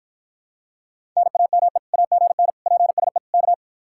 Morse code, heard as a keyed tone, about 54 wpm, 700 Hz. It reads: NAME AGN PSE K